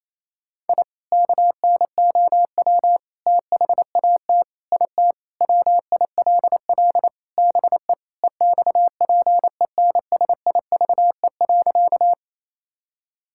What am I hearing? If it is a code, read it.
Morse code, heard as a keyed tone, about 28 wpm, 700 Hz. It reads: I KNOW THAT IT WILL BE EXPENSIVE.
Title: Sentences from Top 1000 Words - Review Part 6 - 28wpm